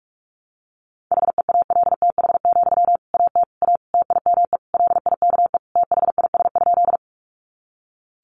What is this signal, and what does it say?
Morse code, heard as a keyed tone, about 44 wpm, 700 Hz. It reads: HEALTH, AT A TIME LIKE THIS?
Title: Sentences from Top 600 Words 45wpm